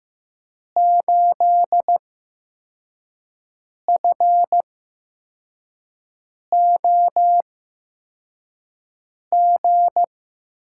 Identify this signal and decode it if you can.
Morse code, heard as a keyed tone, about 15 wpm, spaced as for 4 wpm, 700 Hz. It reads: 8FOG
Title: USA Call Signs - 6 Characters - 8x Character Spacing - 15wpm